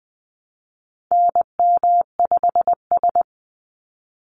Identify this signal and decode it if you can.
Morse code, heard as a keyed tone, about 20 wpm, 700 Hz. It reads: NM5S
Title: NA SOTA Chaser Callsigns 20wpm